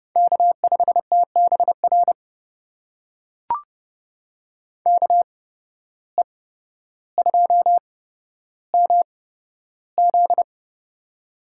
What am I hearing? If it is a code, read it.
Morse code, heard as a keyed tone, about 29 wpm, spaced as for 13 wpm, 700 Hz. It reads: K5TBR KE2MZ
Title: USA Call Signs - 5 Characters - 8x Character Spacing - 30wpm